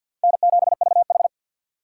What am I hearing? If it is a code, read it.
Morse code, heard as a keyed tone, about 53 wpm, 700 Hz. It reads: N7VH
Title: USA Call Signs - 4 Characters - 3x Character Spacing - 50wpm